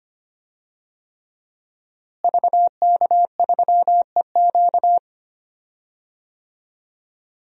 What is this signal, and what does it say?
Morse code, heard as a keyed tone, about 25 wpm, 700 Hz. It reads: VK3EQ
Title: DXCC-100 Call Signs - 5 Characters - Repeated 3x - 25wpm